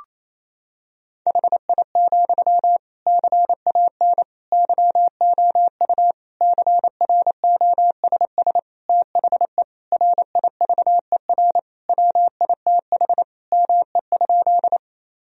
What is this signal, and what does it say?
Morse code, heard as a keyed tone, about 28 wpm, 700 Hz. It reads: HI, CAN YOU CROSS THE RIVER WITH ME?
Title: Sentences from Top 1000 Words - Review Part 3 - 28wpm